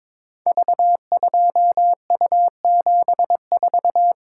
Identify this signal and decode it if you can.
Morse code, heard as a keyed tone, about 22 wpm, 700 Hz. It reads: V2U74
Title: Code Groups - 5 Letter:Numbers - 22wpm